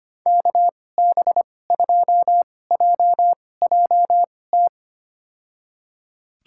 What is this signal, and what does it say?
Morse code, heard as a keyed tone, about 25 wpm, spaced as for 18 wpm, 700 Hz. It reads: KB2JJT
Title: USA Call Signs - 6 Characters - 2x Character Spacing - 25wpm